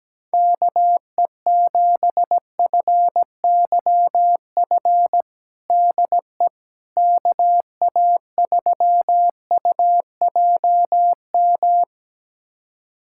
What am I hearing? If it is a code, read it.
Morse code, heard as a keyed tone, about 17 wpm, 700 Hz. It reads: KE7FYF DE KA3UJM
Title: USA Callsign DE Callsign - 6 Characters - 17wpm